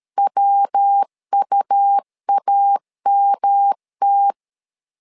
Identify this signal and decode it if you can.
Morse code, heard as a keyed tone, about 13 wpm, 800 Hz. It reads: WUAMT
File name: LEZIONI / 3 MWRO / 3 MW 12wpm 1633